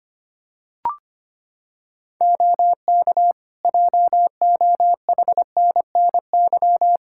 Morse code, heard as a keyed tone, about 25 wpm, 700 Hz. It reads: OK JOHNNY
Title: Practice QSO - Exchange 3 - More Info - 25wpm